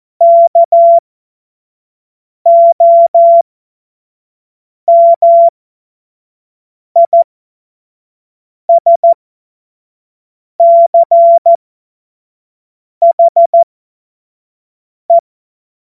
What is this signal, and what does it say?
Morse code, heard as a keyed tone, about 14 wpm, spaced as for 5 wpm, 680 Hz. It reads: KOMISCHE
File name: Morsekurs / Witze / Witz 15. August 2023 25 F70